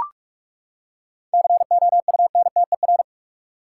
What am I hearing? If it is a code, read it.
Morse code, heard as a keyed tone, about 42 wpm, 700 Hz. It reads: COUNTER
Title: Seven Letter Words - 45wpm